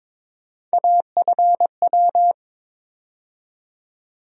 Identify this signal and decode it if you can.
Morse code, heard as a keyed tone, about 22 wpm, 700 Hz. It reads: AFW
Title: USA Call Sign Postfixes - 3 Letters 22wpm